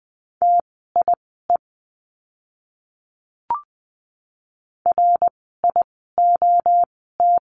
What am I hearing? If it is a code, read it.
Morse code, heard as a keyed tone, about 20 wpm, spaced as for 15 wpm, 700 Hz. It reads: TIE RIOT
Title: 061 Lesson 11f - Letter h - Words 20wpm